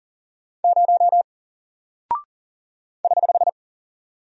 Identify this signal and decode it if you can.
Morse code, heard as a keyed tone, about 40 wpm, 700 Hz. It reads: 0 <HH>
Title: Single Character (Letter:Number:Punctuation:Prosign) Alt 40wpm